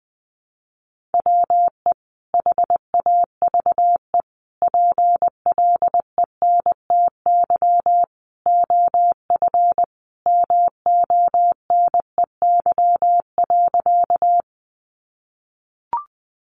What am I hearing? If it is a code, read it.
Morse code, heard as a keyed tone, about 20 wpm, 700 Hz. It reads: WE HAVE PLENTY OF MONEY.
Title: Sentences from Top 1300 Words 20wpm